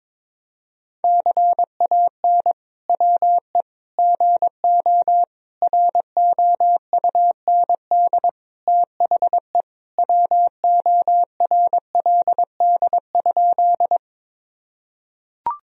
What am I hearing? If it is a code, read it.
Morse code, heard as a keyed tone, about 21 wpm, 700 Hz. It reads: CAN WE GO ROUND THE WORLD?
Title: Sentences from Top 1000 Words - Review Part 1 - 22wpm